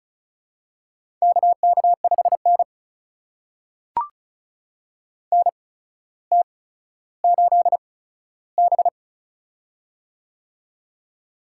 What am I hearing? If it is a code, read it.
Morse code, heard as a keyed tone, about 36 wpm, spaced as for 15 wpm, 700 Hz. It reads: KK5N NT8B
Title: USA Call Signs - 4 Characters - 8x Character Spacing - 35wpm